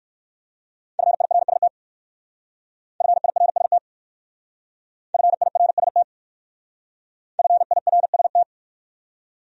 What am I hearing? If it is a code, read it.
Morse code, heard as a keyed tone, about 54 wpm, 700 Hz. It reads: FIRST FIRST FIRST FIRST
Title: Speed-Racing Top 200 Words 45wpm